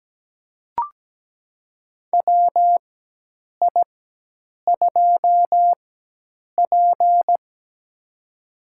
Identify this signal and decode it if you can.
Morse code, heard as a keyed tone, about 17 wpm, spaced as for 8 wpm, 700 Hz. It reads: WI2P
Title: USA Call Signs - 4x Character Spacing - 17wpm